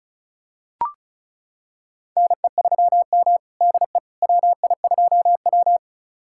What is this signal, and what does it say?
Morse code, heard as a keyed tone, about 36 wpm, 700 Hz. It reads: NE3M DE WI2W
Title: USA Callsign DE Callsign - 4 Characters - 35wpm